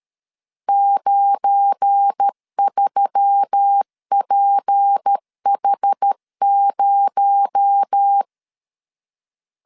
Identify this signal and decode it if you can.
Morse code, heard as a keyed tone, about 13 wpm, 800 Hz. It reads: 93PH0
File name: LEZIONI / 8 YJ45 / 8 YJ4 12wpm 2151